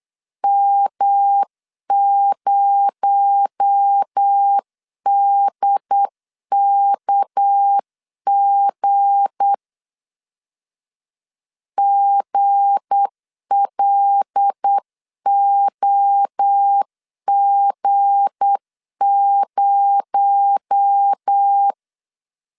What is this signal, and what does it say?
Morse code, heard as a keyed tone, about 8 wpm, 800 Hz. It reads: M0DKG GLOG0
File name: LEZIONI / 7 GF9Z / 7 G 8wpm 3118